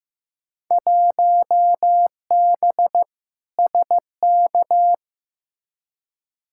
Morse code, heard as a keyed tone, about 15 wpm, 700 Hz. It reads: 1B SK
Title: Field Day Exchanges Encore - 15wpm